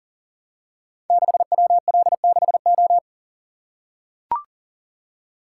Morse code, heard as a keyed tone, about 39 wpm, 700 Hz. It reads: 6WL6O
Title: Code Groups - 5 Letter:Numbers - 40wpm